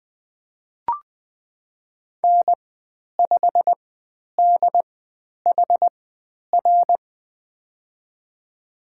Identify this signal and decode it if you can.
Morse code, heard as a keyed tone, about 20 wpm, spaced as for 10 wpm, 700 Hz. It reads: N5DHR
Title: 095 Lesson 16g - Letter c - Call Signs 20wpm